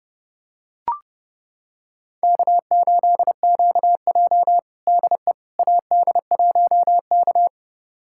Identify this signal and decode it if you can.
Morse code, heard as a keyed tone, about 30 wpm, 700 Hz. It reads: K8QJ DE AD1K
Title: USA Callsign DE Callsign - 4 Characters - 30wpm